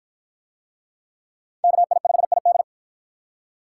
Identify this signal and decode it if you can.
Morse code, heard as a keyed tone, about 52 wpm, 700 Hz. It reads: KI5ID